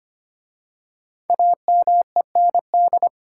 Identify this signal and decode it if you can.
Morse code, heard as a keyed tone, about 25 wpm, 700 Hz. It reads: AMEND
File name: 5 Letter Words - First 3 Letters Given - 25wpm